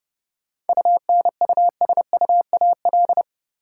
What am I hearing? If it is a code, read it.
Morse code, heard as a keyed tone, about 30 wpm, 700 Hz. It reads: UNUSUAL
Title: Sentences from Top 1400 Words - New Word Drill Alt - 30wpm